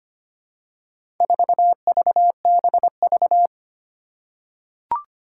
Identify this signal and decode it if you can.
Morse code, heard as a keyed tone, about 25 wpm, 700 Hz. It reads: 4VBV
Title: Four Challenging Letter-Number Combinations 25wpm